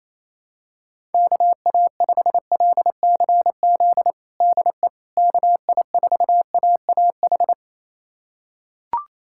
Morse code, heard as a keyed tone, about 28 wpm, 700 Hz. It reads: KA5LCZ DE KI4AAH
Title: USA Callsign DE Callsign - 6 Characters - 28wpm